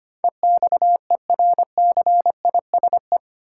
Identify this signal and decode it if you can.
Morse code, heard as a keyed tone, about 25 wpm, 700 Hz. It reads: EXERCISE
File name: Eight Letter Words - 25wpm